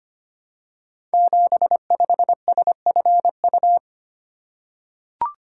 Morse code, heard as a keyed tone, about 25 wpm, 700 Hz. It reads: 75SFU
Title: Code Groups - 5 Letter:Numbers - 25wpm